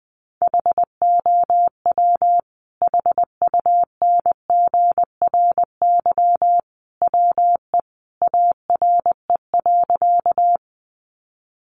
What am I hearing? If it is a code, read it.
Morse code, heard as a keyed tone, about 19 wpm, 700 Hz. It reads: HOW HUNGRY WE ARE.
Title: Sentences from Top 1400 Words 20wpm